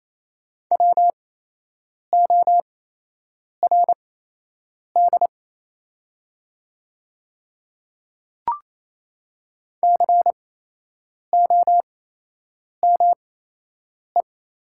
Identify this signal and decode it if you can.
Morse code, heard as a keyed tone, about 28 wpm, spaced as for 8 wpm, 700 Hz. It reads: WORD COME
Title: 8x Character Spacing - Top 100 Words - 28wpm